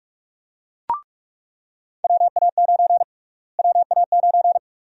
Morse code, heard as a keyed tone, about 46 wpm, 700 Hz. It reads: WA9 WA9
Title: USA Call Sign Prefixes - Repeated 2x - 45wpm